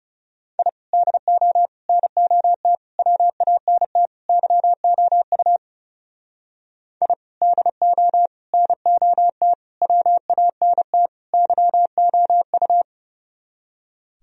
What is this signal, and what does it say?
Morse code, heard as a keyed tone, about 34 wpm, 700 Hz. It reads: I DO NOT WANT YOU I DO NOT WANT YOU
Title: Speed-Racing - Sets of 5 Words 30wpm